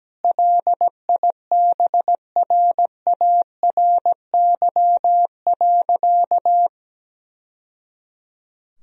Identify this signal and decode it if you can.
Morse code, heard as a keyed tone, about 17 wpm, 700 Hz. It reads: LIBRARY.
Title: Sentences from Top 1000 Words - Review Part 1 - 17wpm